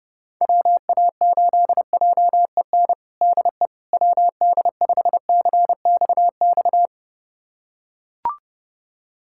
Morse code, heard as a keyed tone, about 31 wpm, 700 Hz. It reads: WA8JEN DE WD5CXX
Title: USA Callsign DE Callsign 30wpm